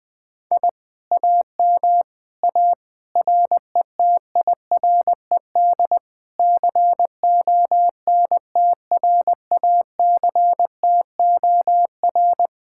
Morse code, heard as a keyed tone, about 20 wpm, 700 Hz. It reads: I AM A RETIRED CONTRACTOR
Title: Speed-Racing Common QSO Phrases - 17wpm